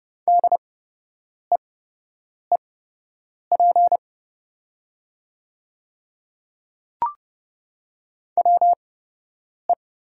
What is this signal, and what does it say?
Morse code, heard as a keyed tone, about 30 wpm, spaced as for 8 wpm, 700 Hz. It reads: DEEP WE
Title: 8x Character Spacing - Top 500 Words - 30wpm